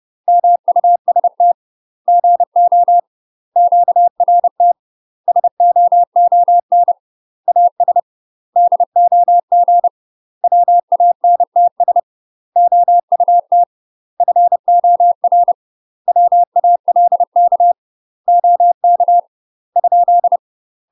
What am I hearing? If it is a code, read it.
Morse code, heard as a keyed tone, about 30 wpm, 700 Hz. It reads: MUST GO QRT SOON AS DOG WANTS OUT FOR WALK OK ?
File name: UK MASTER QSOs TRAINING / 30WPM / UK MASTER QSOs 0035